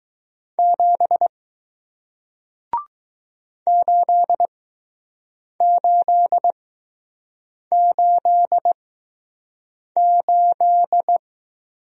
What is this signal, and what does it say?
Morse code, heard as a keyed tone, about 20 wpm, spaced as for 6 wpm, 700 Hz. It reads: 7 8888